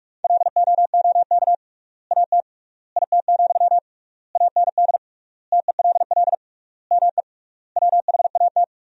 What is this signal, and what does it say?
Morse code, heard as a keyed tone, about 46 wpm, 700 Hz. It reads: LOOK AT IT, AND TELL ME WHAT
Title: 3x Word-Spacing - Sentences from Top 200 Words - 45wpm